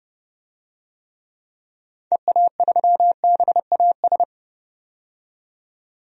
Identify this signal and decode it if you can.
Morse code, heard as a keyed tone, about 30 wpm, 700 Hz. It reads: EA3BAS
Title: DXCC-100 Call Signs - 6 Characters - 30wpm